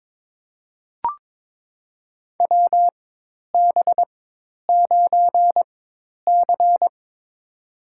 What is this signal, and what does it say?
Morse code, heard as a keyed tone, about 21 wpm, spaced as for 10 wpm, 700 Hz. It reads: WB9C
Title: USA Call Signs - 4 Characters - 4x Character Spacing - 22wpm